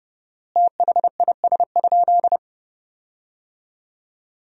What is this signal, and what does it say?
Morse code, heard as a keyed tone, about 30 wpm, 700 Hz. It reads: THIS?
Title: 4x Word-Spacing - Sentences from Top 100 Words - 30wpm